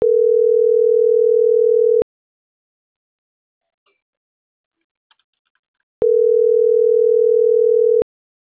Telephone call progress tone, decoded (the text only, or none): ringback tone